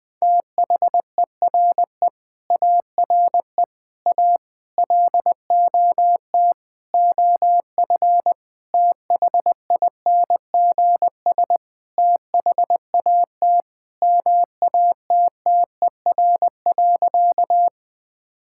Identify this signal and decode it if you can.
Morse code, heard as a keyed tone, about 20 wpm, 700 Hz. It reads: THERE ARE A LOT OF THINGS THAT MATTER.
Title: Sentences with 5-Word Phrases - 20wpm